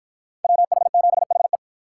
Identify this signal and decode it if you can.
Morse code, heard as a keyed tone, about 50 wpm, 700 Hz. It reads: WH7HE